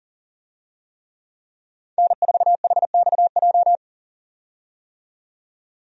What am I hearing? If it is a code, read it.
Morse code, heard as a keyed tone, about 40 wpm, 700 Hz. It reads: N4HXJ
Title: USA Call Signs - 5 Characters - 40wpm